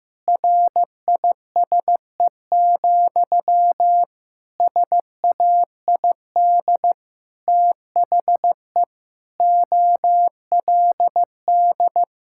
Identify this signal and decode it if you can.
Morse code, heard as a keyed tone, about 15 wpm, 700 Hz. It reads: RISE, SAID THE OLD
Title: Sentences from Top 1000 Words - Review Part 6 - 15wpm